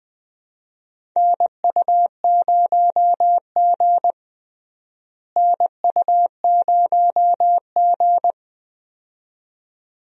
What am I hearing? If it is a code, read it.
Morse code, heard as a keyed tone, about 20 wpm, 700 Hz. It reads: NU0G NU0G
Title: USA Call Signs - 4 Characters - Repeated 2x - 20wpm